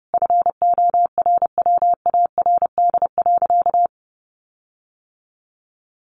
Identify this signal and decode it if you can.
Morse code, heard as a keyed tone, about 30 wpm, 700 Hz. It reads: FORWARD.
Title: Sentences from Top 1300 Words 30wpm